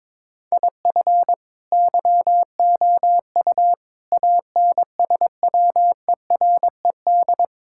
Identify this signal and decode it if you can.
Morse code, heard as a keyed tone, about 22 wpm, 700 Hz. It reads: IF YOU ANSWERED
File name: Sets of 9 Words - 22wpm